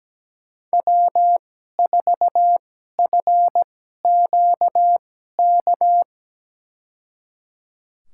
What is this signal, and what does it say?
Morse code, heard as a keyed tone, about 17 wpm, spaced as for 12 wpm, 700 Hz. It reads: W4FQK
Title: USA Call Signs - 5 Characters - 2x Character Spacing - 17wpm